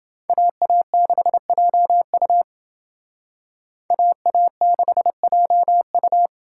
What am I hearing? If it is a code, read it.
Morse code, heard as a keyed tone, about 31 wpm, 700 Hz. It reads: AA6JU AA6JU